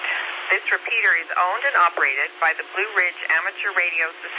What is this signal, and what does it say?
Emergency Alert System (SAME) tones, heard as harsh buzzing bursts, with speech. It is heard from 0.4 to 3.8 s.